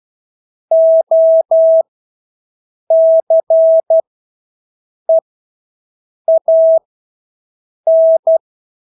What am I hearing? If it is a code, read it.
Morse code, heard as a keyed tone, about 12 wpm, spaced as for 6 wpm, 650 Hz. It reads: OCEAN
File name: Engelske ord 12-6 wpm